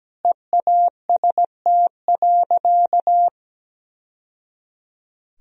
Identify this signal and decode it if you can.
Morse code, heard as a keyed tone, about 17 wpm, 700 Hz. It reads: EAST.
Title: Sentences from Top 1000 Words - Review Part 3 - 17wpm